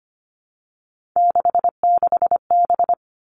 Morse code, heard as a keyed tone, about 25 wpm, 700 Hz. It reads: 66B
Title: All Cut Numbers - 3-Digit 25wpm